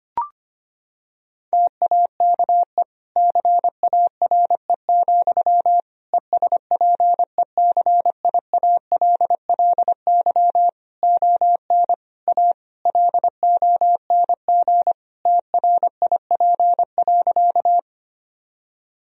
Morse code, heard as a keyed tone, about 25 wpm, 700 Hz. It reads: TAKE CARE, ESPECIALLY ON A LONG TRIP.
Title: Sentences from Top 1000 Words - Review Part 4 - 25wpm